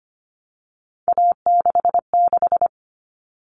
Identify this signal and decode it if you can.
Morse code, heard as a keyed tone, about 25 wpm, 700 Hz. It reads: A66
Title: All Cut Numbers - 3-Digit 25wpm